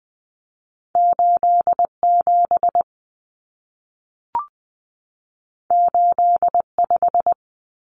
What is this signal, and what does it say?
Morse code, heard as a keyed tone, about 20 wpm, 700 Hz. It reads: 87 85
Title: Common Cut Numbers - 2-Digit 20wpm